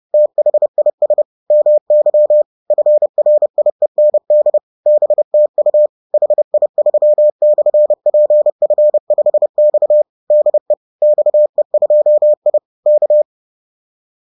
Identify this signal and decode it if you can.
Morse code, heard as a keyed tone, about 30 wpm, 600 Hz. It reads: THIS MY FRIEND BTU HI3/PF5X DE XE2I K